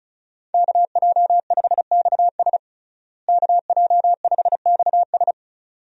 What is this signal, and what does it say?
Morse code, heard as a keyed tone, about 36 wpm, 700 Hz. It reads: KJ5XS KJ5XS
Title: USA Call Signs - 5 Characters - Repeated 2x - 35wpm